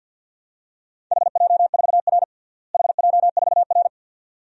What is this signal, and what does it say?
Morse code, heard as a keyed tone, about 52 wpm, 700 Hz. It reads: HJ4R HJ4R
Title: DXCC-100 Call Signs - 4 Characters - Repeated 2x - 50wpm